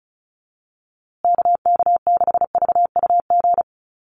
Morse code, heard as a keyed tone, about 34 wpm, 700 Hz. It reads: KK6VUG
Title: USA Call Signs Repeated 3x 35wpm